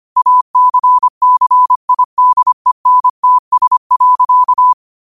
Morse code, heard as a keyed tone, about 25 wpm, 1000 Hz. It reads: ACCIDENTS.
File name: qotdfiles / 25wpm